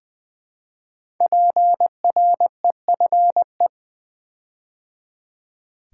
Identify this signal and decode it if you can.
Morse code, heard as a keyed tone, about 20 wpm, 700 Hz. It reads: PREFE
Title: Build A Word - 5 Letters At A Time - 10-Letter Words - 20wpm